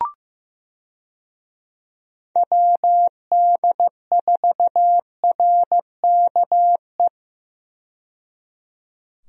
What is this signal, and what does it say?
Morse code, heard as a keyed tone, about 15 wpm, 700 Hz. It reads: WD4RKE
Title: USA Call Signs - 6 Characters - 15wpm